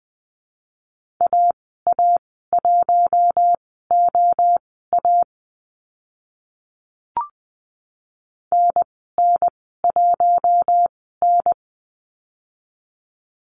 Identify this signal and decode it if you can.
Morse code, heard as a keyed tone, about 20 wpm, spaced as for 14 wpm, 700 Hz. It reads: AA1OA NN1N
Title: 042 Lesson 8g - Number 1 - Call Signs 20wpm